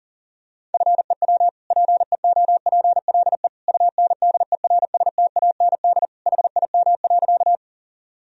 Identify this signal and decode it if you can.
Morse code, heard as a keyed tone, about 40 wpm, 700 Hz. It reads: FEW PEOPLE UNDERSTAND HIM.
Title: Sentences from Top 1000 Words - Review Part 1 - 40wpm